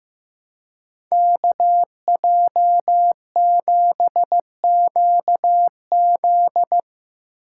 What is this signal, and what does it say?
Morse code, heard as a keyed tone, about 17 wpm, 700 Hz. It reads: KJ7QZ
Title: Speed Racing - USA 2x2 Call Signs 15wpm